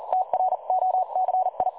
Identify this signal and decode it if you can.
Morse code, heard as a keyed tone, about 39 wpm, 740 Hz. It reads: TROCI
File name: dlazaawansowanych / nagraniazpasma / websdr recording 3565.0kHz